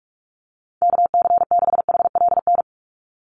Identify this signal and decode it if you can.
Morse code, heard as a keyed tone, about 44 wpm, 700 Hz. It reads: KC6HLN
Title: USA Call Signs 45wpm